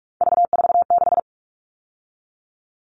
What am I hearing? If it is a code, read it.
Morse code, heard as a keyed tone, about 44 wpm, 700 Hz. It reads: V46
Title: All Cut Numbers - 3-Digit 45wpm